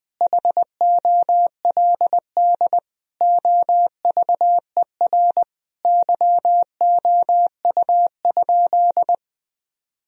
Morse code, heard as a keyed tone, about 21 wpm, 700 Hz. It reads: HOLD OVER YOU?
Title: Sentences from Top 1000 Words - Review Part 2 - 20wpm